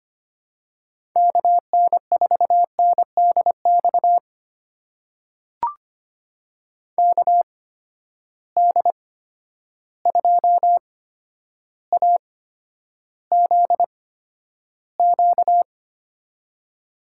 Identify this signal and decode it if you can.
Morse code, heard as a keyed tone, about 25 wpm, spaced as for 11 wpm, 700 Hz. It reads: KN4NDX KD2AZQ
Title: USA Call Signs - 6 Characters - 8x Character Spacing - 25wpm